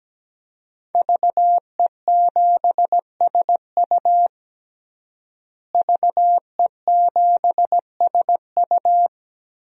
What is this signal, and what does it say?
Morse code, heard as a keyed tone, about 17 wpm, 700 Hz. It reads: VE7SU VE7SU
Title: DXCC-100 Call Signs - 5 Characters - Repeated 2x - 17wpm